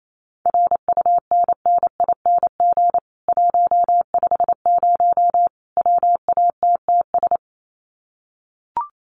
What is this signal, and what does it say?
Morse code, heard as a keyed tone, about 28 wpm, 700 Hz. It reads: RUNNING 150 WATTS
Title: Common QSO Phrases 28wpm